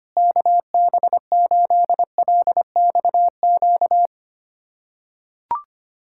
Morse code, heard as a keyed tone, about 25 wpm, 700 Hz. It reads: KB8LXQ
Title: USA Callsign DE Callsign - 6 Characters - 25wpm